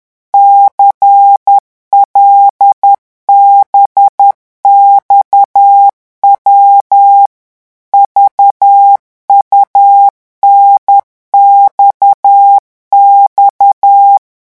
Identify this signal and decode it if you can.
Morse code, heard as a keyed tone, about 10 wpm, 800 Hz. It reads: CLBXW VUNXX